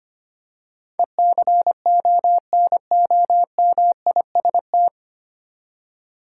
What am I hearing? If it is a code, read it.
Morse code, heard as a keyed tone, about 25 wpm, 700 Hz. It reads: ECONOMIST
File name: Nine Letter Words - 25wpm